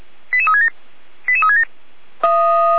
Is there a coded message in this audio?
Yes: five-tone selective call (ZVEI) from 0.3 to 1.6 s, heard as quick runs of five tones, tones of about 70 ms. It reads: 88367 88367